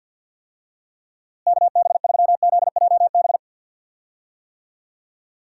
Morse code, heard as a keyed tone, about 48 wpm, 700 Hz. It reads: KB3ZJB
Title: USA Call Signs - 6 Characters - Repeated 3x - 50wpm